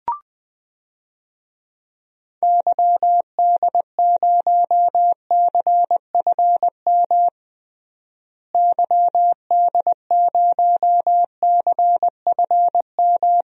Morse code, heard as a keyed tone, about 21 wpm, 700 Hz. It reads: YD0CFM YD0CFM
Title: DXCC-100 Call Signs - 6 Characters - Repeated 3x - 20wpm